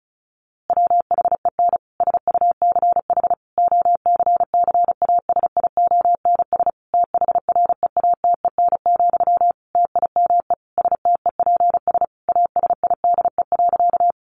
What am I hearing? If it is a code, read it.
Morse code, heard as a keyed tone, about 34 wpm, 700 Hz. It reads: WHEN SUCH OCCASIONS THREATEN, TIME STEPS ASIDE.
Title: A Sound Of Thunder 35wpm